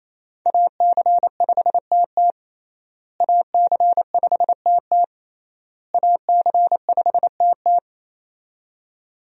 Morse code, heard as a keyed tone, about 28 wpm, 700 Hz. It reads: AC5TT AC5TT AC5TT
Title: USA Call Signs - 5 Characters - Repeated 3x - 28wpm